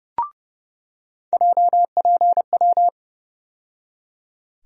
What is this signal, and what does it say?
Morse code, heard as a keyed tone, about 31 wpm, 700 Hz. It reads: JPW
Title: USA Call Sign Postfixes - 3 Letters 30wpm